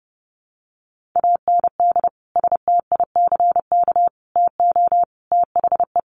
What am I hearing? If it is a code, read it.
Morse code, heard as a keyed tone, about 30 wpm, 700 Hz. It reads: AND STICK TO THE
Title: Sentences from Top 1200 Words 30wpm